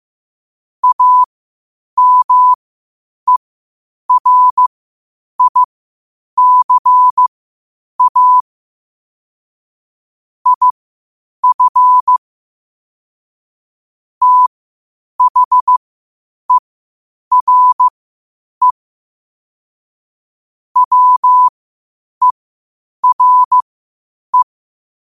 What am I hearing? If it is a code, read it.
Morse code, heard as a keyed tone, about 15 wpm, spaced as for 8 wpm, 1000 Hz. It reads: AMERICA IF THERE WERE
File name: qotdfiles / 8wpm